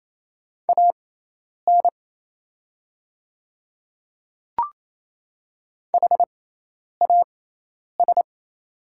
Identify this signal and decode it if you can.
Morse code, heard as a keyed tone, about 27 wpm, spaced as for 10 wpm, 700 Hz. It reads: AN HAS
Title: 6x Character Spacing - Top 100 Words - 28wpm